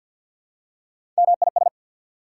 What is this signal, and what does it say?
Morse code, heard as a keyed tone, about 53 wpm, 690 Hz. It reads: MIS
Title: Prefixes and Suffixes 50wpm